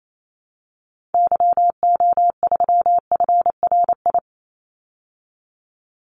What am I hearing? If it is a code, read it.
Morse code, heard as a keyed tone, about 28 wpm, 700 Hz. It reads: YO3FRI
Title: Call Signs Repeated 2x - 28wpm